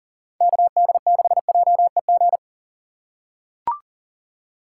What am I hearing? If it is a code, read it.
Morse code, heard as a keyed tone, about 40 wpm, 700 Hz. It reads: KD6JEG
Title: USA Call Signs - 6 Characters - 2x Character Spacing - 40wpm